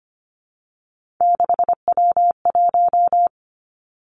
Morse code, heard as a keyed tone, about 25 wpm, 700 Hz. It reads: 6W1
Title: Three Challenging Letter-Number Combinations 25wpm